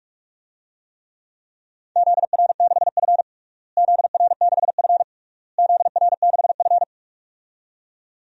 Morse code, heard as a keyed tone, about 41 wpm, 700 Hz. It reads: ZR6F ZR6F ZR6F
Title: DXCC-100 Call Signs - 4 Characters - Repeated 3x - 45wpm